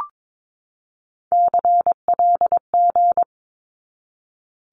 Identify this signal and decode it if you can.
Morse code, heard as a keyed tone, about 22 wpm, 700 Hz. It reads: CLG
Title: Abbreviations 22wpm